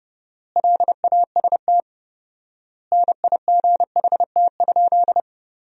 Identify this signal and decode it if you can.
Morse code, heard as a keyed tone, about 32 wpm, 700 Hz. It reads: LAST NIGHT?
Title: 4x Word-Spacing - Sentences from Top 300 Words - 30wpm